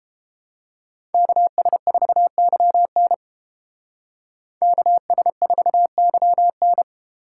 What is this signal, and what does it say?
Morse code, heard as a keyed tone, about 28 wpm, 700 Hz. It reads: KS4YN KS4YN